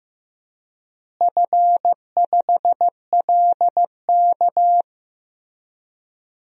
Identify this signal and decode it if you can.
Morse code, heard as a keyed tone, about 15 wpm, 700 Hz. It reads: F5LK
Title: DXCC-100 Call Signs - 4 Characters - Repeated 3x - 15wpm